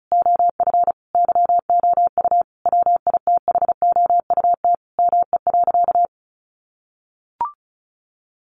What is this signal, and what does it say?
Morse code, heard as a keyed tone, about 34 wpm, 700 Hz. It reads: OF YOU WITHOUT ME.
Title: Sentences from Top 1200 Words 35wpm